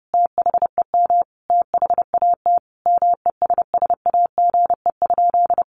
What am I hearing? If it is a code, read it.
Morse code, heard as a keyed tone, about 30 wpm, 700 Hz. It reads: THEM THAT MESSAGE?
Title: Sentences from Top 1100 Words 30wpm